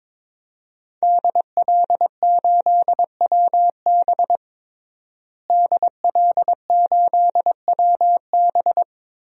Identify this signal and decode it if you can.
Morse code, heard as a keyed tone, about 23 wpm, 700 Hz. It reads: DL8WB DL8WB
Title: DXCC-100 Call Signs - 5 Characters - Repeated 2x - 22wpm